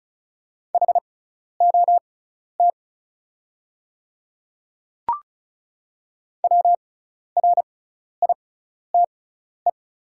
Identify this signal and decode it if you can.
Morse code, heard as a keyed tone, about 36 wpm, spaced as for 12 wpm, 700 Hz. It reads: HOT WRITE
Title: 6x Character Spacing - Top 100 Words - 35wpm